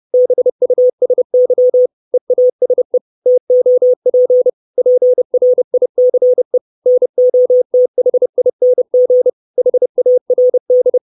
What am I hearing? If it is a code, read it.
Morse code, heard as a keyed tone, about 30 wpm, 500 Hz. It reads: BUSY EASE TOP PRICE NOTHING HARD
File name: 632 Words 30WPM